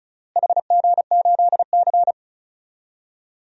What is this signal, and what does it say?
Morse code, heard as a keyed tone, about 34 wpm, 700 Hz. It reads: HG8C
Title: DXCC-100 Call Signs - 4 Characters - Repeated 3x - 35wpm